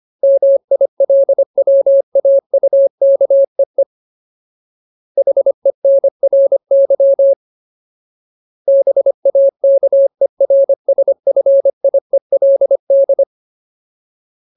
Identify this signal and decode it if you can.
Morse code, heard as a keyed tone, about 25 wpm, 550 Hz. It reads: MILWAUKEE HENRY BAKERSFIELD
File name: Ultimate Word List / 25WPM Ultimate